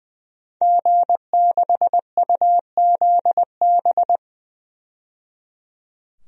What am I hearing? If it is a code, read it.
Morse code, heard as a keyed tone, about 21 wpm, 700 Hz. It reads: G6UZB
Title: DXCC-100 Call Signs - 5 Characters - 20wpm